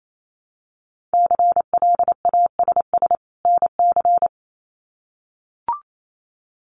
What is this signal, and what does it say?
Morse code, heard as a keyed tone, about 28 wpm, 700 Hz. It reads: CLASS NC
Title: Winter Field Day Exchanges 28wpm